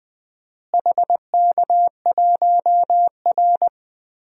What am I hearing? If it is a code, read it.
Morse code, heard as a keyed tone, about 20 wpm, 700 Hz. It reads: HK1R
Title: DXCC-100 Call Signs - 4 Characters - Repeated 3x - 20wpm